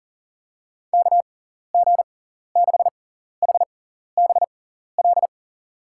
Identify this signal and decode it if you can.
Morse code, heard as a keyed tone, about 40 wpm, spaced as for 14 wpm, 700 Hz. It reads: KG6HBL
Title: USA Call Signs - 6x Character Spacing - 40wpm